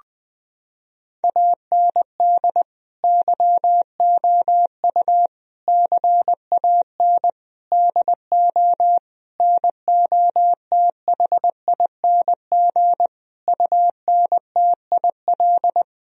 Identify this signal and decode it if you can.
Morse code, heard as a keyed tone, about 20 wpm, 700 Hz. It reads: AND YOU CAN DO NOTHING UNTIL
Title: Sentences from Top 1000 Words - Review Part 1 - 20wpm